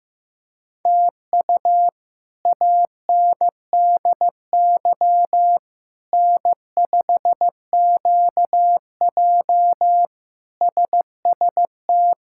Morse code, heard as a keyed tone, about 15 wpm, 700 Hz. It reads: TU ANDY N5QJ SST
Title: K1USN SST QSO Practice - 15wpm